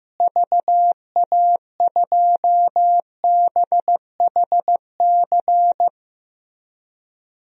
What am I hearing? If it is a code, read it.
Morse code, heard as a keyed tone, about 16 wpm, 700 Hz. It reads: VA2BHC
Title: DXCC-100 Call Signs - 6 Characters - Repeated 2x - 15wpm